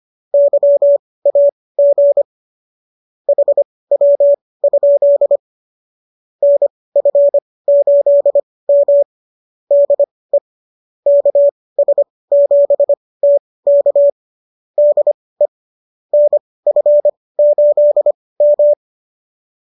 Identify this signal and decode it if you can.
Morse code, heard as a keyed tone, about 25 wpm, spaced as for 18 wpm, 570 Hz. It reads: YAG HW? NF8M DE KS7T K DE NF8M